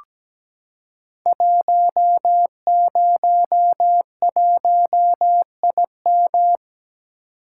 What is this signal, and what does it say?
Morse code, heard as a keyed tone, about 17 wpm, 700 Hz. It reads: 101IM
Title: Code Groups - 5 Letter:Numbers - 17wpm